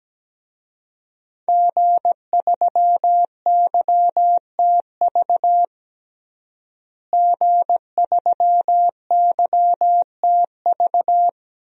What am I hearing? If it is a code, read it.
Morse code, heard as a keyed tone, about 17 wpm, 700 Hz. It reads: G3YTV G3YTV
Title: DXCC-100 Call Signs - 5 Characters - Repeated 3x - 17wpm